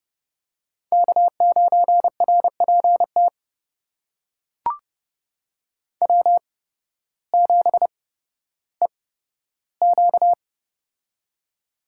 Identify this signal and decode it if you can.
Morse code, heard as a keyed tone, about 30 wpm, spaced as for 14 wpm, 700 Hz. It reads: K9RPT W7EQ
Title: USA Call Signs - 8x Character Spacing - 30wpm